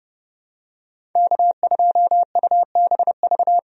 Morse code, heard as a keyed tone, about 30 wpm, 700 Hz. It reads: K2UBV
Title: USA Call Signs - 5 Characters - 30wpm